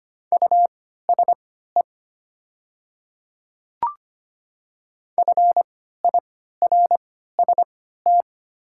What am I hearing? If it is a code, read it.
Morse code, heard as a keyed tone, about 25 wpm, spaced as for 14 wpm, 700 Hz. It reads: USE FIRST
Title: 3x Character Spacing - Top 100 Words - 25wpm